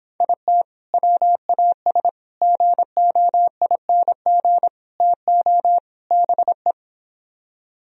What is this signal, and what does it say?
Morse code, heard as a keyed tone, about 26 wpm, 700 Hz. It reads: IT WAS GOING TO BE